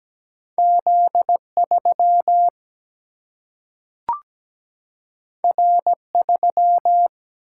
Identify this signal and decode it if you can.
Morse code, heard as a keyed tone, about 17 wpm, 700 Hz. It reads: Z3 R3
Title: Single-Letter Single-Number 17wpm